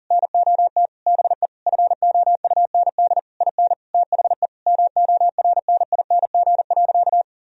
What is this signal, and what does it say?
Morse code, heard as a keyed tone, about 40 wpm, 700 Hz. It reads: NOT BE FOUND IN THE MORNING.
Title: Sentences from Top 1000 Words - Review Part 1 - 40wpm